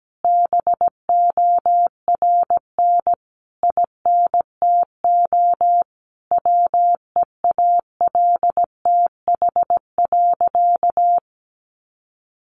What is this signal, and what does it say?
Morse code, heard as a keyed tone, about 17 wpm, 700 Hz. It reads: BORN INTO WEALTH.
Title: Sentences from Top 1400 Words 17wpm